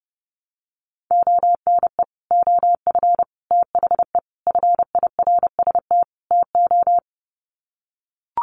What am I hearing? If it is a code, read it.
Morse code, heard as a keyed tone, about 30 wpm, 700 Hz. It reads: ONE OF THE FIRST TO
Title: Sets of 5 Words Encore 30wpm